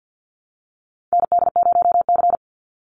Morse code, heard as a keyed tone, about 49 wpm, 700 Hz. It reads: ND0C
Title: NA SOTA Chaser Callsigns 50wpm